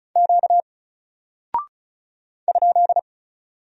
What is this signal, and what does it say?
Morse code, heard as a keyed tone, about 36 wpm, 700 Hz. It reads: Q ?
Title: Single Character (Letter:Number:Punctuation:Prosign) 35wpm